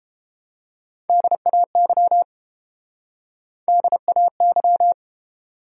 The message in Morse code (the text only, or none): DAY DAY